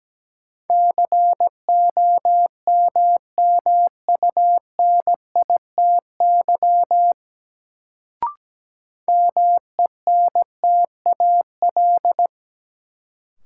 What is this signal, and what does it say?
Morse code, heard as a keyed tone, about 17 wpm, 700 Hz. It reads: COMMUNITY MENTAL
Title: Sentences from Top 1000 Words - Review Part 4 - New Word Drill - 17wpm